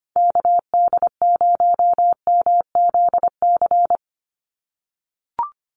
Morse code, heard as a keyed tone, about 25 wpm, 700 Hz. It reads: KD0MZC
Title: USA Call Signs 25wpm